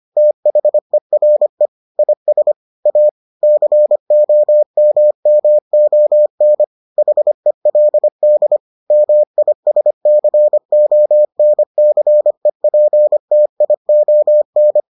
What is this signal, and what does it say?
Morse code, heard as a keyed tone, about 25 wpm, 600 Hz. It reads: THERE IS A COMMON HELD MISCONCEPTION